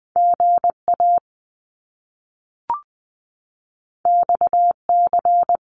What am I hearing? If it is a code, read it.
Morse code, heard as a keyed tone, about 20 wpm, 700 Hz. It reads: GA XC
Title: Top Two Letter Combinations in Words 20wpm